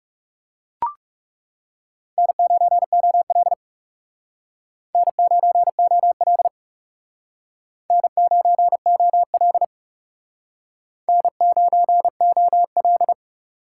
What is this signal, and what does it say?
Morse code, heard as a keyed tone, about 37 wpm, 700 Hz. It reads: N9OL N9OL N9OL N9OL
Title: Speed Racing - USA 1x2 Call Signs 30wpm